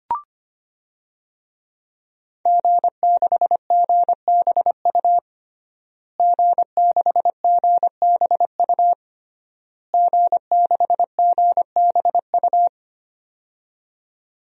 Morse code, heard as a keyed tone, about 25 wpm, 700 Hz. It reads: G6GBU G6GBU G6GBU
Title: DXCC-100 Call Signs - 5 Characters - Repeated 3x - 25wpm